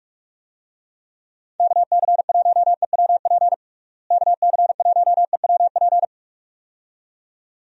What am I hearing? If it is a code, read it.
Morse code, heard as a keyed tone, about 46 wpm, 700 Hz. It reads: KC1EWP KC1EWP
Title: USA Call Signs - 6 Characters - Repeated 2x - 45wpm